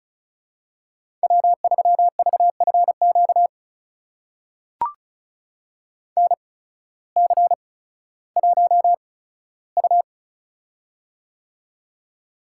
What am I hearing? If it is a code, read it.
Morse code, heard as a keyed tone, about 36 wpm, spaced as for 16 wpm, 700 Hz. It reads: W3VFQ NC1U
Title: USA Call Signs - 8x Character Spacing - 35wpm